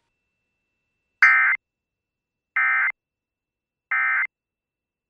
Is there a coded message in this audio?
Yes: Emergency Alert System (SAME) tones from 1.2 to 4.3 s, heard as harsh buzzing bursts.